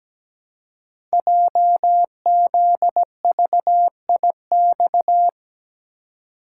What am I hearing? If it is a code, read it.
Morse code, heard as a keyed tone, about 17 wpm, 700 Hz. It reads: JZVIX
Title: Code Groups - 5 Letters - 17wpm